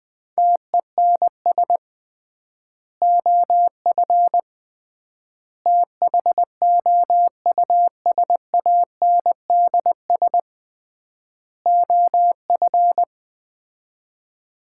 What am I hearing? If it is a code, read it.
Morse code, heard as a keyed tone, about 20 wpm, 700 Hz. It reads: TENS OF THOUSANDS OF
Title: 3x Word-Spacing - Sets of 4 Words - 20wpm